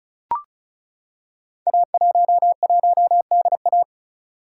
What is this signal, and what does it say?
Morse code, heard as a keyed tone, about 36 wpm, 700 Hz. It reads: A11DA
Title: Code Groups - 5 Letter:Numbers - 35wpm